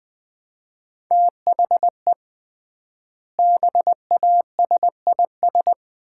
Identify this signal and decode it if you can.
Morse code, heard as a keyed tone, about 20 wpm, 700 Hz. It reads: THE BASIS
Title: 3x Word-Spacing - Sets of 3 Words - 20wpm